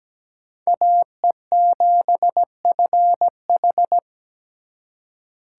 Morse code, heard as a keyed tone, about 20 wpm, 700 Hz. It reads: AE7FH
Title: Speed Racing - USA 2x2 Call Signs 17wpm